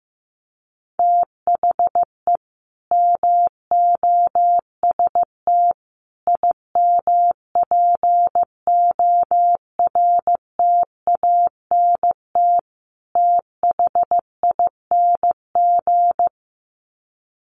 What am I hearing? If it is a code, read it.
Morse code, heard as a keyed tone, about 15 wpm, 700 Hz. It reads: THE MOST IMPORTANT THING
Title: Sets of 4 Words 15wpm